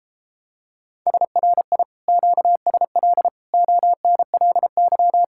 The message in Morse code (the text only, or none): SRI QSL ONLY